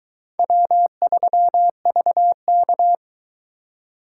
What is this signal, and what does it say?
Morse code, heard as a keyed tone, about 22 wpm, 700 Hz. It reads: W3VK